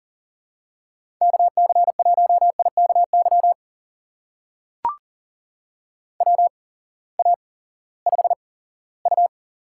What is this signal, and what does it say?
Morse code, heard as a keyed tone, about 40 wpm, spaced as for 20 wpm, 700 Hz. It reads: KC1IKY WA5U
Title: USA Call Signs - 6 Characters - 8x Character Spacing - 40wpm